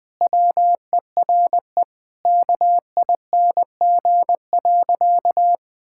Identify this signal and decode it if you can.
Morse code, heard as a keyed tone, about 21 wpm, 700 Hz. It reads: WERE KING.
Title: Sentences with 4-Word Phrases - Spoken Beginning - 20wpm